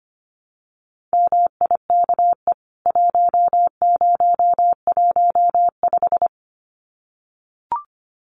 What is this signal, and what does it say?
Morse code, heard as a keyed tone, about 25 wpm, 700 Hz. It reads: MIKE 1015
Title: CWT Exchange 25wpm